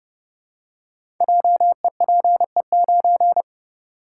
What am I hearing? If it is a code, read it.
Morse code, heard as a keyed tone, about 29 wpm, 700 Hz. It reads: JEPE9
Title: Code Groups - 5 Letter:Numbers - 30wpm